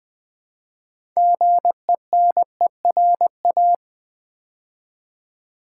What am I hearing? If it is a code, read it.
Morse code, heard as a keyed tone, about 20 wpm, 700 Hz. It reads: GENERA
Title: Build A Word - 2 Letters At A Time - 8-Letter Words - 20wpm